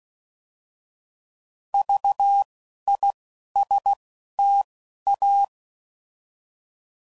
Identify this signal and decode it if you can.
Morse code, heard as a keyed tone, about 16 wpm, 800 Hz. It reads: V I S T A
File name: Prefixes / By First Letter / Vs